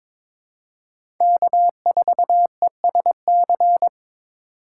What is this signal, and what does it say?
Morse code, heard as a keyed tone, about 22 wpm, 700 Hz. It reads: K4ESC
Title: USA Call Signs - 5 Characters - 4x Character Spacing - 22wpm